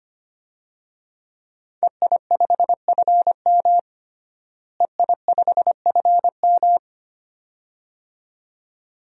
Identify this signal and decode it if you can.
Morse code, heard as a keyed tone, about 25 wpm, 700 Hz. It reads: EI5FM EI5FM
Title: DXCC-100 Call Signs - 5 Characters - Repeated 2x - 25wpm